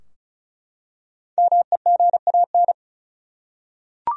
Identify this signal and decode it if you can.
Morse code, heard as a keyed tone, about 36 wpm, 700 Hz. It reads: MEGAN
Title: USA Common Names - Max 5 Letters - 35wpm